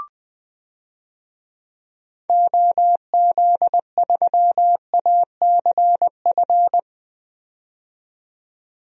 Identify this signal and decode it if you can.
Morse code, heard as a keyed tone, about 20 wpm, 700 Hz. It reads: OZ3ACF
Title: DXCC-100 Call Signs - 6 Characters - 20wpm